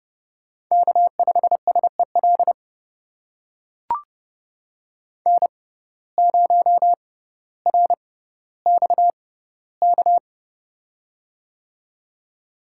Morse code, heard as a keyed tone, about 30 wpm, spaced as for 15 wpm, 700 Hz. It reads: K5SEL N0RXK
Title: USA Call Signs - 6x Character Spacing - 30wpm